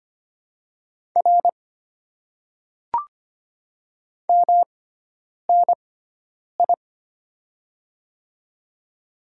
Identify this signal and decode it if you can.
Morse code, heard as a keyed tone, about 25 wpm, spaced as for 9 wpm, 700 Hz. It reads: R MNI
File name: Top 90 QSO Elements - 6x Character Spacing - 25wpm